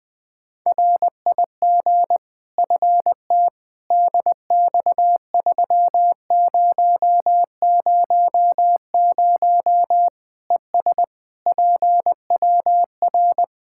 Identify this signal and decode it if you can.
Morse code, heard as a keyed tone, about 20 wpm, 700 Hz. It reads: RIG FT DX3000 ES PWR
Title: Practice QSO - Exchange 2 - Info - 20wpm